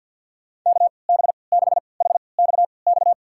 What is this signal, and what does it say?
Morse code, heard as a keyed tone, about 46 wpm, spaced as for 27 wpm, 690 Hz. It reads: KB6HXX
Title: USA Call Signs - 6 Characters - 3x Character Spacing - 50wpm